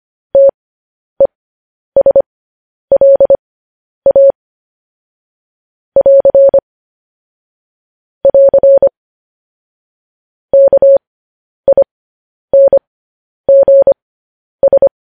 Morse code, heard as a keyed tone, about 25 wpm, spaced as for 10 wpm, 550 Hz. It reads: TESLA <AR> <AR> KINGS